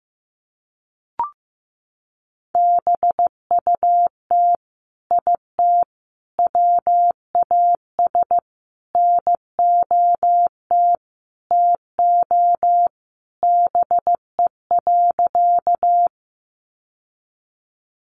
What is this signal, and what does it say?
Morse code, heard as a keyed tone, about 15 wpm, 700 Hz. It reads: BUT IT WAS NOT TO BE.
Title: Sentences from Top 600 Words 15wpm